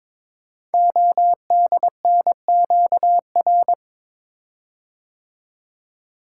Morse code, heard as a keyed tone, about 22 wpm, 700 Hz. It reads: ODNQR
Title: Code Groups - 5 Letters - 22wpm